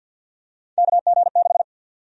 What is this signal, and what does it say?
Morse code, heard as a keyed tone, about 50 wpm, 700 Hz. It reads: KG6
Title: USA Call Sign Prefixes - 50wpm